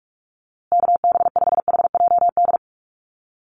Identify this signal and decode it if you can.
Morse code, heard as a keyed tone, about 44 wpm, 700 Hz. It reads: KB5HJD
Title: USA Call Signs Repeated 2x 45wpm